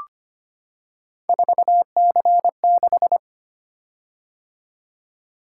Morse code, heard as a keyed tone, about 25 wpm, 700 Hz. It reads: 4C6
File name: Three Letter-Number Combinations - 25wpm